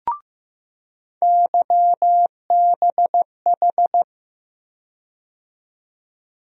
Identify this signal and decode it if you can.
Morse code, heard as a keyed tone, about 15 wpm, 700 Hz. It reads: YBH
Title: USA Call Sign Postfixes - 3 Letters 15wpm